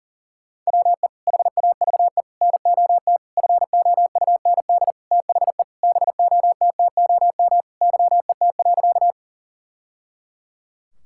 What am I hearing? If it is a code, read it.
Morse code, heard as a keyed tone, about 40 wpm, 700 Hz. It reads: WE HAVE NOT FOUND THE BOTTOM YET.
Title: Sentences from Top 1000 Words - Review Part 6 - 40wpm